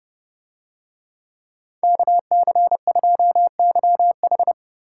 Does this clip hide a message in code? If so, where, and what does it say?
Yes: Morse code from 1.8 to 4.5 s, heard as a keyed tone, about 29 wpm, 700 Hz. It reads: KC2YH